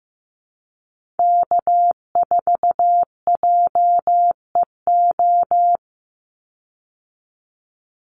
Morse code, heard as a keyed tone, about 15 wpm, 700 Hz. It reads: K4JEO
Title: USA Call Signs 15wpm